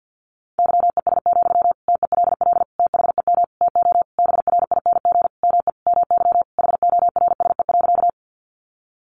Morse code, heard as a keyed tone, about 50 wpm, 700 Hz. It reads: YES, TELL THEM TO BRING ME MY HORSE.
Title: Sentences from Top 500 Words 50wpm